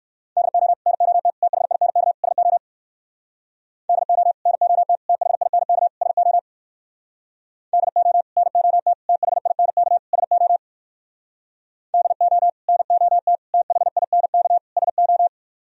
Morse code, heard as a keyed tone, about 59 wpm, 700 Hz. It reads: DO NOT THINK SO DO NOT THINK SO DO NOT THINK SO DO NOT THINK SO